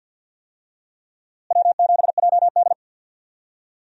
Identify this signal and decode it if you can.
Morse code, heard as a keyed tone, about 52 wpm, 700 Hz. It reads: W7JD
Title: USA Call Signs - 4 Characters - Repeated 3x - 50wpm